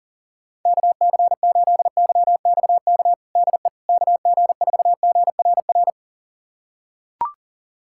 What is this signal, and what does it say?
Morse code, heard as a keyed tone, about 40 wpm, 700 Hz. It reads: KC8YXK DE KG4GRR
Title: USA Callsign DE Callsign - 6 Characters - 40wpm